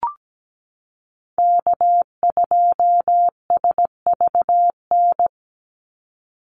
Morse code, heard as a keyed tone, about 17 wpm, 700 Hz. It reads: K2SVN
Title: USA Call Signs 17wpm